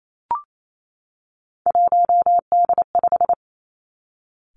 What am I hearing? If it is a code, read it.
Morse code, heard as a keyed tone, about 28 wpm, 700 Hz. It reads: 1D5
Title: Three Challenging Letter-Number Combinations 28wpm